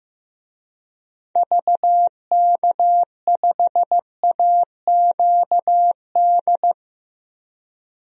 Morse code, heard as a keyed tone, about 15 wpm, 700 Hz. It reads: VK5AQD
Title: DXCC-100 Call Signs - 6 Characters - 15wpm